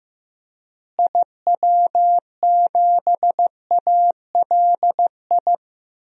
Morse code, heard as a keyed tone, about 15 wpm, 700 Hz. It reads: IW7ALI
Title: DXCC-100 Call Signs - 6 Characters - Repeated 3x - 15wpm